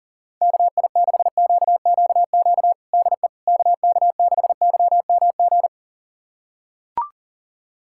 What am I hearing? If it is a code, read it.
Morse code, heard as a keyed tone, about 40 wpm, 700 Hz. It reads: KI6QQQ DE KK6YMG
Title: USA Callsign DE Callsign 40wpm